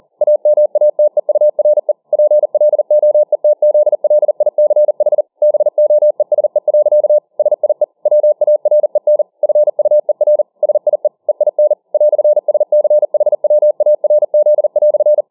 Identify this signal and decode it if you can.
Morse code, heard as a keyed tone, about 40 wpm, 600 Hz. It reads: AMATEURE PLOETZLICH BOESE. SIE WAREN FUER SIE EIN "SCHWARZ"